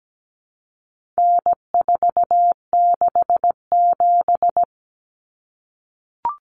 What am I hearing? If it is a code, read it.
Morse code, heard as a keyed tone, about 17 wpm, 700 Hz. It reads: N467
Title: Common Cut Numbers - 4-Digit 17wpm